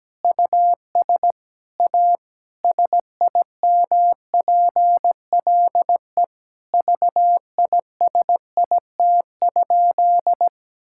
Morse code, heard as a keyed tone, about 17 wpm, 700 Hz. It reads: US A SIMPLE VISIT?
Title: Sentences from Top 1000 Words - Review Part 6 - 17wpm